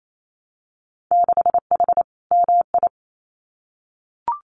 Morse code, heard as a keyed tone, about 28 wpm, 700 Hz. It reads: 6H MI
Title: Winter Field Day Exchanges Encore 28wpm